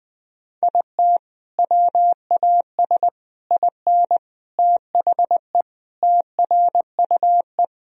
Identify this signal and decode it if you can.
Morse code, heard as a keyed tone, about 20 wpm, 700 Hz. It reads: IT WAS IN THE TRUE
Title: Sentences from Top 1000 Words - Review Part 4 - 20wpm